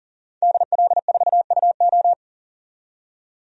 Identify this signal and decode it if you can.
Morse code, heard as a keyed tone, about 40 wpm, 700 Hz. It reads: DL4UO
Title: DXCC-100 Call Signs - 5 Characters - 40wpm